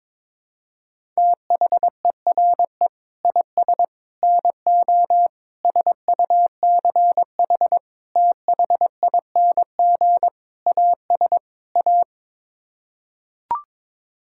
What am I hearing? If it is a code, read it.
Morse code, heard as a keyed tone, about 22 wpm, 700 Hz. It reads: THERE IS NO SUCH THING AS A
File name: Sets of 7 Words - 22wpm